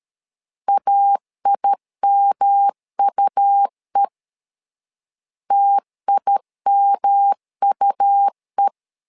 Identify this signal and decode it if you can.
Morse code, heard as a keyed tone, about 13 wpm, 800 Hz. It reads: AIMUE TIMUE
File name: LEZIONI / 3 MWRO / 3 M 12wpm 1541